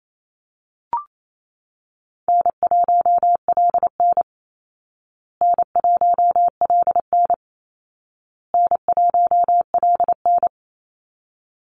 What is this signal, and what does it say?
Morse code, heard as a keyed tone, about 28 wpm, 700 Hz. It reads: N1LN N1LN N1LN
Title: Easy Call Signs Repeated 3x - 28wpm